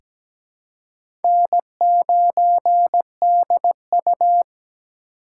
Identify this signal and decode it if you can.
Morse code, heard as a keyed tone, about 17 wpm, 700 Hz. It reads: N9DU
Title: USA Call Signs - 4x Character Spacing - 17wpm